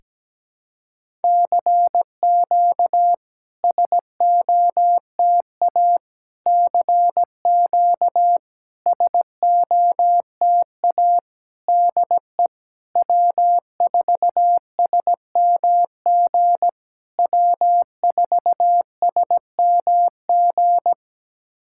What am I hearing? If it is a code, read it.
Morse code, heard as a keyed tone, about 17 wpm, 690 Hz. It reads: CQ SOTA CQ SOTA DE W4SMG W4SMG
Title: SOTA QSO Practice 17wpm